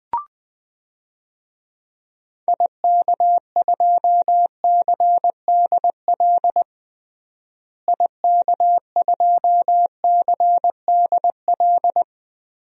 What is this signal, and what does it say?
Morse code, heard as a keyed tone, about 20 wpm, 700 Hz. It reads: IK2CDL IK2CDL
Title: DXCC-100 Call Signs - 6 Characters - Repeated 3x - 20wpm